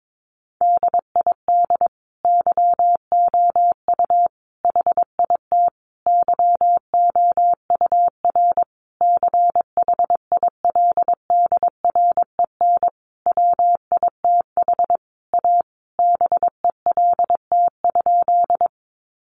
Morse code, heard as a keyed tone, about 22 wpm, 700 Hz. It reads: DID YOU HIT YOUR CHILDREN WITH A BELT?
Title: Sentences from Top 1400 Words 22wpm